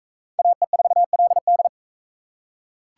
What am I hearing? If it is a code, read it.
Morse code, heard as a keyed tone, about 41 wpm, 700 Hz. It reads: AE4LD